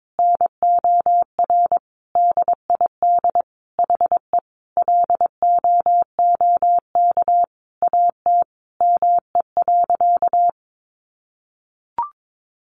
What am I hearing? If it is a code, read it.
Morse code, heard as a keyed tone, about 22 wpm, 700 Hz. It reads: NOR DID HE LOOK AT ME.
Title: Sentences from Top 1500 Words 22wpm